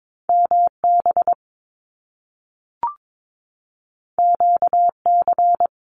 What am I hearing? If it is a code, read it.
Morse code, heard as a keyed tone, about 22 wpm, 700 Hz. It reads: MB QC
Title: Canadian Provinces and Territories 22wpm